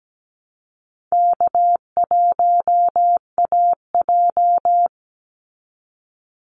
Morse code, heard as a keyed tone, about 17 wpm, 700 Hz. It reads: K1AJ
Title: Call Signs Repeated 3x - 17wpm